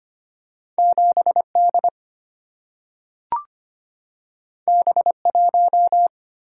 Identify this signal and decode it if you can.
Morse code, heard as a keyed tone, about 25 wpm, 700 Hz. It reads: 7D B1
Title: Single-Letter Single-Number 25wpm